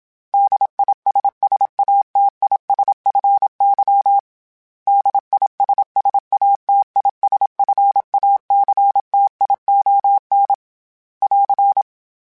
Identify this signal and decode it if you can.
Morse code, heard as a keyed tone, about 27 wpm, 800 Hz. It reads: DISSATISFY DISSATISFACTION <AR>
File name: Special INGs / Special INGs-mp3s / Suffixes @25